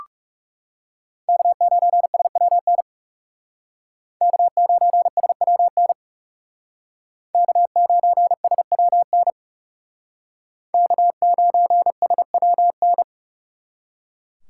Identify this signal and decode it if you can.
Morse code, heard as a keyed tone, about 38 wpm, 700 Hz. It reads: K9SWN K9SWN K9SWN K9SWN